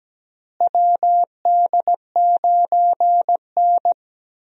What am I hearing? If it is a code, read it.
Morse code, heard as a keyed tone, about 17 wpm, 700 Hz. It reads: WD9N
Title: USA Call Signs - 4 Characters - Repeated 3x - 17wpm